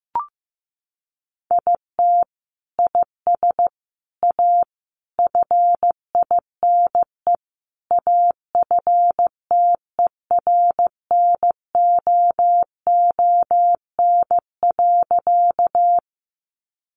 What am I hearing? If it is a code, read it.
Morse code, heard as a keyed tone, about 15 wpm, 700 Hz. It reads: IT IS A FINE AFTERNOON.
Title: Sentences from Top 1500 Words 15wpm